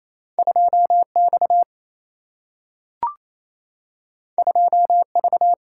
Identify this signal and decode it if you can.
Morse code, heard as a keyed tone, about 28 wpm, 700 Hz. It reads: 2X 2V
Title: Single-Letter Single-Number 28wpm